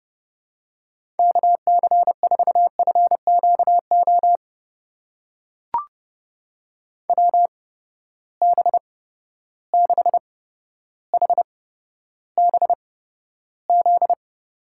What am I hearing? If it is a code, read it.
Morse code, heard as a keyed tone, about 31 wpm, spaced as for 13 wpm, 700 Hz. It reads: KC4FQO WB6HBZ
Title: USA Call Signs - 6 Characters - 8x Character Spacing - 30wpm